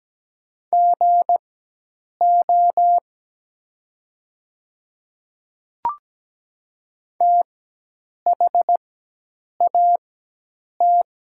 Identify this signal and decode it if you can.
Morse code, heard as a keyed tone, about 17 wpm, spaced as for 8 wpm, 700 Hz. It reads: GO THAT
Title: 4x Character Spacing - Top 200 Words - 17wpm